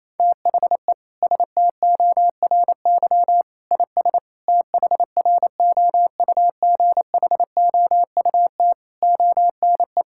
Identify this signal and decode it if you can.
Morse code, heard as a keyed tone, about 28 wpm, 700 Hz. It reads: THE STORY IS THROUGHOUT ONE
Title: Sentences from Top 1000 Words - Review Part 5 - 28wpm